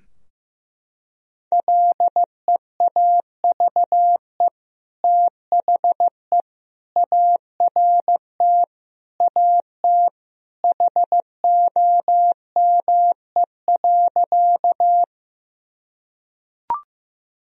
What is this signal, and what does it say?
Morse code, heard as a keyed tone, about 15 wpm, 700 Hz. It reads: LEAVE THE ART AT HOME.
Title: Sentences from Top 1000 Words - Review Part 3 - 15wpm